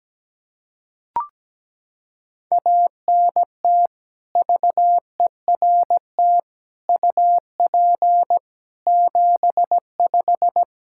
Morse code, heard as a keyed tone, about 17 wpm, 700 Hz. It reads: ANT VERT UP 75
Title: K1USN SST QSO Practice 17wpm